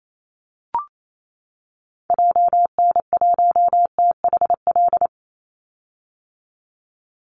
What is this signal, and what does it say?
Morse code, heard as a keyed tone, about 28 wpm, 700 Hz. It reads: JN1THL
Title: Call Signs - 28wpm